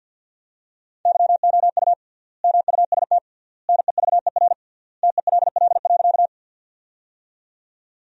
Morse code, heard as a keyed tone, about 47 wpm, 700 Hz. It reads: YOU MUST NEVER TELL.
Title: 3x Word-Spacing - Sentences from Top 300 Words - 50wpm